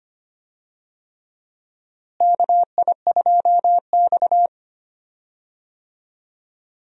Morse code, heard as a keyed tone, about 25 wpm, 700 Hz. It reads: KI2X
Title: USA Call Signs - 4 Characters - Repeated 3x - 25wpm